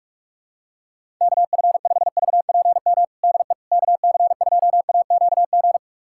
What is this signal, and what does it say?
Morse code, heard as a keyed tone, about 42 wpm, 700 Hz. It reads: KF5VPM DE KC2AQG
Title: USA Callsign DE Callsign - 6 Characters - 45wpm